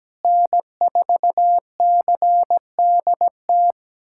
Morse code, heard as a keyed tone, about 17 wpm, 700 Hz. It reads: N4CDT
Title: USA Callsign DE Callsign - 5 Characters - 17wpm